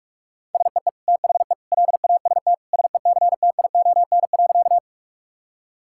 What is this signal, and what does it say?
Morse code, heard as a keyed tone, about 41 wpm, 700 Hz. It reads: SEE THE LAST SECTION.
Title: Sentences from Top 1000 Words - Review Part 5 - 45wpm